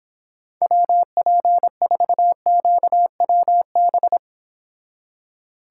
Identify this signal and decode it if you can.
Morse code, heard as a keyed tone, about 26 wpm, 700 Hz. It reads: WP4QWB